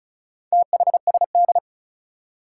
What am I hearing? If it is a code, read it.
Morse code, heard as a keyed tone, about 36 wpm, 700 Hz. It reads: THSD
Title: Four Challenging Letter-Number Combinations 35wpm